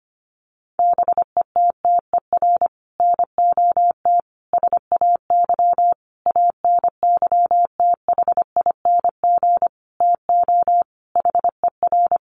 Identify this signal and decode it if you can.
Morse code, heard as a keyed tone, about 25 wpm, 700 Hz. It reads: BETTER NOT SAY ANYTHING TO HER
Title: Sentences from Top 1000 Words 25wpm